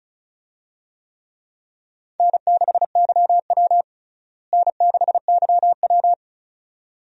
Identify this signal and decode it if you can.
Morse code, heard as a keyed tone, about 36 wpm, 700 Hz. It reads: N6YW N6YW
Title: USA Call Signs - 4 Characters - Repeated 2x - 35wpm